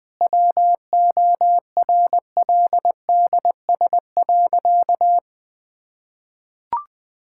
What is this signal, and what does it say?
Morse code, heard as a keyed tone, about 20 wpm, 700 Hz. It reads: WORLDS.
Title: Difficult Sentences with Hints - 20wpm